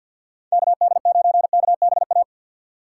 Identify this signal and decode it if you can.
Morse code, heard as a keyed tone, about 53 wpm, 700 Hz. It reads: KD9KBA
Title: USA Call Signs - 3x Character Spacing - 50wpm